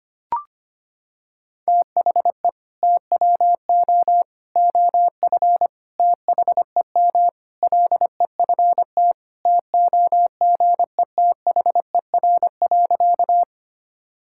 Morse code, heard as a keyed tone, about 25 wpm, 700 Hz. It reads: THE TWO OF THEM LEFT TOGETHER.
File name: practice / 25wpm / Sentences with 4-Word Phrases - 25wpm